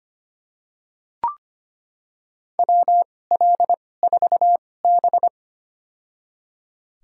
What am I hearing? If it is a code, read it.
Morse code, heard as a keyed tone, about 25 wpm, spaced as for 18 wpm, 700 Hz. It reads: WL4B
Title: USA Call Signs - 4 Characters - 2x Character Spacing - 25wpm